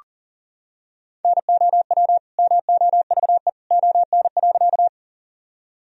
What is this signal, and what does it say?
Morse code, heard as a keyed tone, about 40 wpm, 700 Hz. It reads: NOW MOVE ON.
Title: Sentences from Top 1000 Words - Review Part 1 - 40wpm